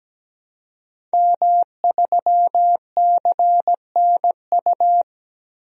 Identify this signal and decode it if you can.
Morse code, heard as a keyed tone, about 17 wpm, 700 Hz. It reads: M3CNU
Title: DXCC-100 Call Signs - 5 Characters - Repeated 3x - 17wpm